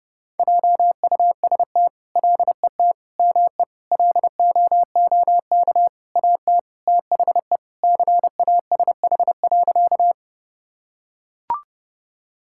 Morse code, heard as a keyed tone, about 32 wpm, 700 Hz. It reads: JUST LET ME LOOK AT THE CASH.
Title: Sentences from Top 1000 Words - Review Part 5 - 30wpm